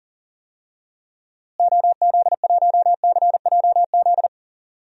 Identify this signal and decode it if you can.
Morse code, heard as a keyed tone, about 39 wpm, 700 Hz. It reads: OZ1CJZ
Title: DXCC-100 Call Signs - 6 Characters - Repeated 3x - 40wpm